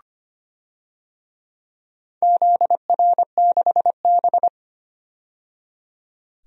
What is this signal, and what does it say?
Morse code, heard as a keyed tone, about 25 wpm, 700 Hz. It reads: ZR6B
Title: DXCC-100 Call Signs - 4 Characters - 25wpm